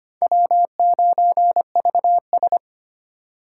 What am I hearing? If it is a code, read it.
Morse code, heard as a keyed tone, about 31 wpm, 700 Hz. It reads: W9VS